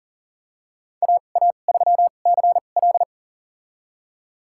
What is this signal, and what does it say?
Morse code, heard as a keyed tone, about 40 wpm, spaced as for 29 wpm, 700 Hz. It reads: AA3CL
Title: USA Call Signs - 5 Characters - 2x Character Spacing - 40wpm